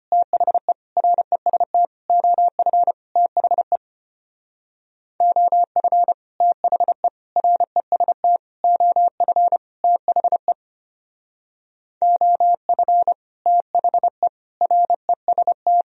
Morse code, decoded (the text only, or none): THE REST OF THE OF THE REST OF THE OF THE REST